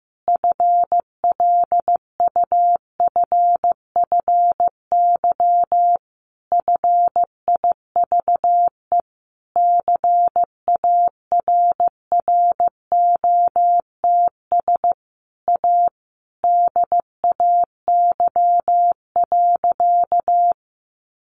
Morse code, heard as a keyed tone, about 15 wpm, 700 Hz. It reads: FLUFFY FIVE CARROTS A DAY.
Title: Easy Phrases 15wpm